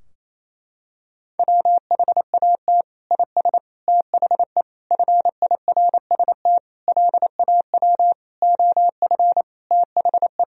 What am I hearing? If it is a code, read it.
Morse code, heard as a keyed tone, about 28 wpm, 700 Hz. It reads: WHAT IS THE FIRST LAW OF THE
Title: Sentences from Top 1000 Words - Review Part 3 - 28wpm